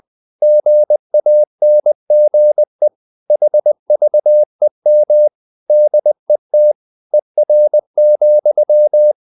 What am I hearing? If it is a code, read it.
Morse code, heard as a keyed tone, about 20 wpm, 600 Hz. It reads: GANGE HVEM DET ER,
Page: Ferie 20 wpm